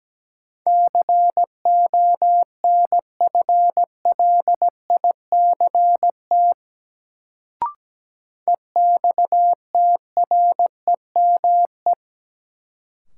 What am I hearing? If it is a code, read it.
Morse code, heard as a keyed tone, about 17 wpm, 700 Hz. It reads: CONFLICT EXTREME
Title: Sentences from Top 1300 Words - New Word Drill - 17wpm